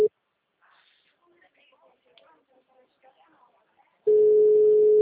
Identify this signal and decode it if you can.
Ringback tone (Europe, 425 Hz), with speech.